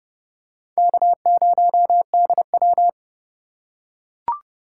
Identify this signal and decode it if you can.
Morse code, heard as a keyed tone, about 28 wpm, 700 Hz. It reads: K0DW